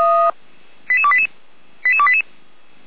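Five-tone selective call (ZVEI), heard as quick runs of five tones, tones of about 70 ms. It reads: 88288 88288
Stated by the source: Five-tone ZVEI selective call, SDR receiver, 455 channel, feuerwehr-orlamuende fire brigade.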